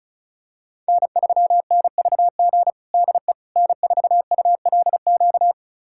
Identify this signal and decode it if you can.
Morse code, heard as a keyed tone, about 34 wpm, 700 Hz. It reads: N3NVG DE N4ULQ
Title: USA Callsign DE Callsign - 5 Characters - 35wpm